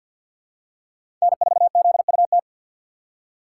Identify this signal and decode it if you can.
Morse code, heard as a keyed tone, about 48 wpm, 690 Hz. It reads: N4ZUT
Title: USA Call Signs - 5 Characters - 6x Character Spacing - 50wpm